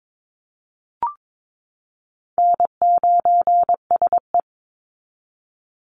Morse code, heard as a keyed tone, about 22 wpm, 700 Hz. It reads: N9SE
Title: Easy Call Signs Repeated 3x - 22wpm